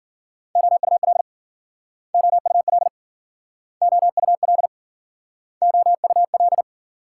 Morse code, heard as a keyed tone, about 53 wpm, 700 Hz. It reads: OUL OUL OUL OUL